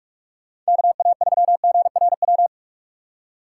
Morse code, heard as a keyed tone, about 56 wpm, 700 Hz. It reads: KA3GRW
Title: Speed Racing - USA 2x3 Call Signs 45wpm